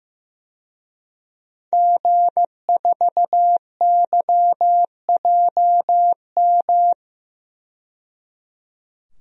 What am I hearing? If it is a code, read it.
Morse code, heard as a keyed tone, about 16 wpm, 700 Hz. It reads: G4YJM
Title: DXCC-100 Call Signs - 5 Characters - 15wpm